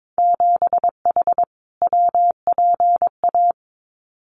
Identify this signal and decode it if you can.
Morse code, heard as a keyed tone, about 22 wpm, 700 Hz. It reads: 7H WPA
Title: Winter Field Day Exchanges Encore 22wpm